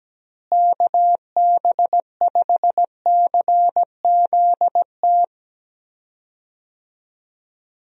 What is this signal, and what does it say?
Morse code, heard as a keyed tone, about 17 wpm, 700 Hz. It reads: KB5CZT
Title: USA Call Signs - 6 Characters - Repeated 3x - 17wpm